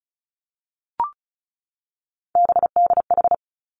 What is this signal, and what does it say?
Morse code, heard as a keyed tone, about 36 wpm, 700 Hz. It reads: BDH
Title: Three Challenging Letter-Number Combinations 35wpm